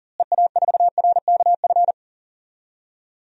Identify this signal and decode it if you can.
Morse code, heard as a keyed tone, about 40 wpm, 700 Hz. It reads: EA4RKF
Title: DXCC-100 Call Signs - 6 Characters - Repeated 3x - 40wpm